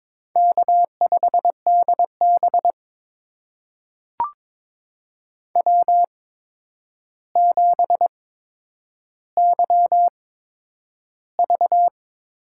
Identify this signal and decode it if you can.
Morse code, heard as a keyed tone, about 23 wpm, spaced as for 10 wpm, 700 Hz. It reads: K5DB W7YV
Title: USA Call Signs - 8x Character Spacing - 22wpm